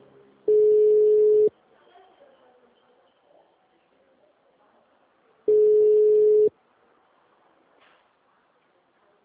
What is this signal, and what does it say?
Ringback tone (Europe, 425 Hz), with speech.
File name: out- -2011- - - .63